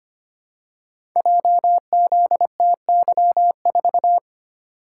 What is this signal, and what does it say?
Morse code, heard as a keyed tone, about 25 wpm, 700 Hz. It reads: JZTY4
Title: Code Groups - 5 Letter:Numbers - 25wpm